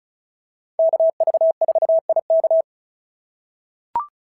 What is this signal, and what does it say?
Morse code, heard as a keyed tone, about 36 wpm, 690 Hz. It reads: KV4IK
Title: SOTA QSO Practice 35wpm